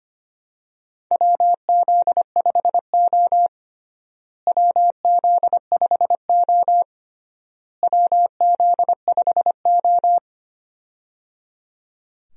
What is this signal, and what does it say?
Morse code, heard as a keyed tone, about 25 wpm, 700 Hz. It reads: WZ5O WZ5O WZ5O
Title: USA Call Signs - 4 Characters - Repeated 3x - 25wpm